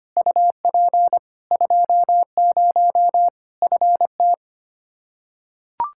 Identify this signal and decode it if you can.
Morse code, heard as a keyed tone, about 25 wpm, 700 Hz. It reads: UP 20 FT
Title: Practice QSO - Exchange 2 - Info - 25wpm